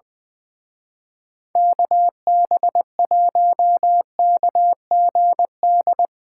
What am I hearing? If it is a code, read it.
Morse code, heard as a keyed tone, about 20 wpm, 700 Hz. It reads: KB1KGD
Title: Speed Racing - USA 2x3 Call Signs 20wpm